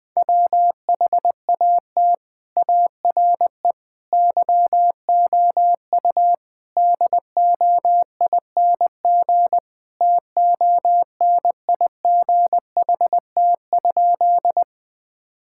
Morse code, heard as a keyed tone, about 20 wpm, 700 Hz. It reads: WHAT ARE YOU DOING TONIGHT?
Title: Sentences with 4-Word Phrases - Spoken Beginning - 20wpm